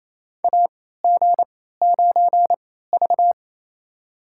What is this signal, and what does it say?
Morse code, heard as a keyed tone, about 28 wpm, spaced as for 16 wpm, 700 Hz. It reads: AG9V
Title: USA Call Signs - 4 Characters - 3x Character Spacing - 28wpm